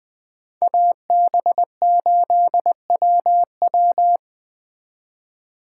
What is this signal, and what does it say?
Morse code, heard as a keyed tone, about 21 wpm, 700 Hz. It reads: AB8WW
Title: Speed Racing - USA 2x2 Call Signs 17wpm